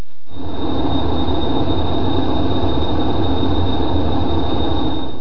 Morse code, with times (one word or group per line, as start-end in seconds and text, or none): none